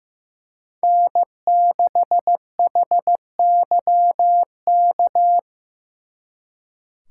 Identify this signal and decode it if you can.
Morse code, heard as a keyed tone, about 18 wpm, 700 Hz. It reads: N6HYK
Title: Speed Racing - USA 1x3 Call Signs 15wpm